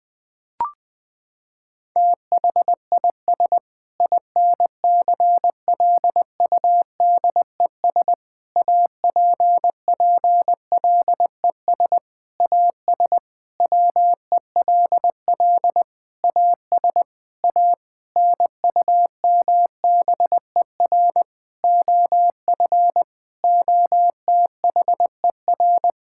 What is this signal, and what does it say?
Morse code, heard as a keyed tone, about 20 wpm, 700 Hz. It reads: THIS INCLUDES APPLES AS WELL AS A NUMBER OF OTHER
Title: Sentences with 7-Word Phrases - 20wpm